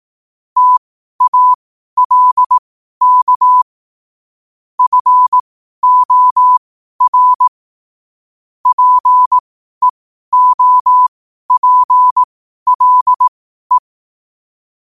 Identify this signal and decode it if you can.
Morse code, heard as a keyed tone, about 18 wpm, spaced as for 12 wpm, 1000 Hz. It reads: TALK FOR PEOPLE
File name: qotdfiles / 12wpm